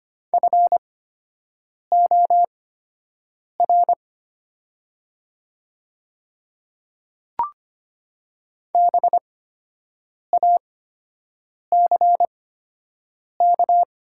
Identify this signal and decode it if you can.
Morse code, heard as a keyed tone, about 25 wpm, spaced as for 7 wpm, 700 Hz. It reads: FOR BACK
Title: 8x Character Spacing - Top 200 Words - 25wpm